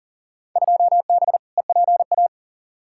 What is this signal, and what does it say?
Morse code, heard as a keyed tone, about 40 wpm, 700 Hz. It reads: 2B EPA
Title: Field Day Exchanges Encore - 40wpm